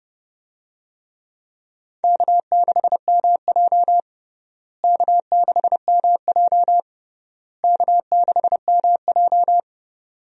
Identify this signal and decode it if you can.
Morse code, heard as a keyed tone, about 29 wpm, 700 Hz. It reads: K6MJ K6MJ K6MJ
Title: USA Call Signs - 4 Characters - Repeated 3x - 30wpm